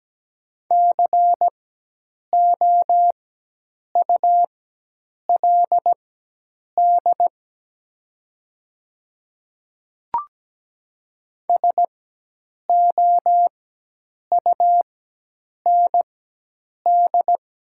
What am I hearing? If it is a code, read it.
Morse code, heard as a keyed tone, about 17 wpm, spaced as for 8 wpm, 700 Hz. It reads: COULD SOUND
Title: 4x Character Spacing - Top 100 Words - 17wpm